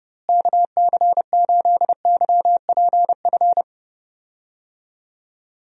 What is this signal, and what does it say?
Morse code, heard as a keyed tone, about 30 wpm, 700 Hz. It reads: KC8YPF
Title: USA Call Signs - 6 Characters - Repeated 3x - 30wpm